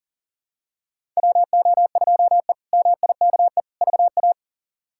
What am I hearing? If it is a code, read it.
Morse code, heard as a keyed tone, about 40 wpm, 700 Hz. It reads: WO2E MIKE VA
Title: K1USN SST QSO Practice - 40wpm